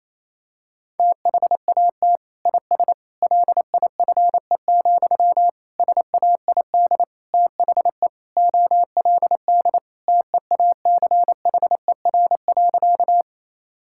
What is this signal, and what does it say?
Morse code, heard as a keyed tone, about 28 wpm, 700 Hz. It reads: THAT IS LIFE, SAID THE OLD TEACHER.
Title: Sentences from Top 1000 Words - Review Part 6 - 28wpm